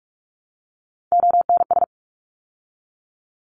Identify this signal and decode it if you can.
Morse code, heard as a keyed tone, about 44 wpm, 700 Hz. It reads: ONS
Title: ARRL Sections 45wpm